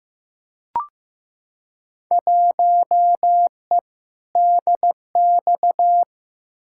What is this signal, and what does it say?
Morse code, heard as a keyed tone, about 15 wpm, 700 Hz. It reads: 1E DX
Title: Field Day Exchanges Encore - 15wpm